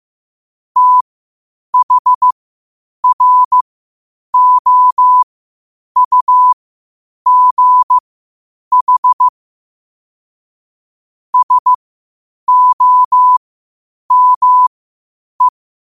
Morse code, heard as a keyed tone, about 15 wpm, spaced as for 8 wpm, 1000 Hz. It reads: THROUGH SOME
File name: qotdfiles / 8wpm